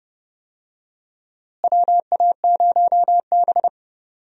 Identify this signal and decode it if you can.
Morse code, heard as a keyed tone, about 30 wpm, 700 Hz. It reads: WA0B
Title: USA Call Signs - 4 Characters - Repeated 3x - 30wpm